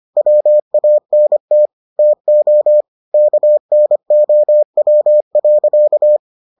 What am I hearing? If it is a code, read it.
Morse code, heard as a keyed tone, about 25 wpm, 600 Hz. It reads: WANT TO KNOW.